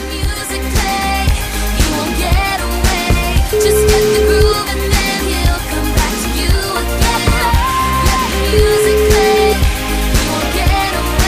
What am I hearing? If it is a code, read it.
Ringback tone (Europe, 425 Hz), with noise.